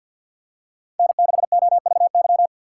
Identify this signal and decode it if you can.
Morse code, heard as a keyed tone, about 50 wpm, 700 Hz. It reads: N6OVY
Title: USA Call Signs - 5 Characters - 2x Character Spacing - 50wpm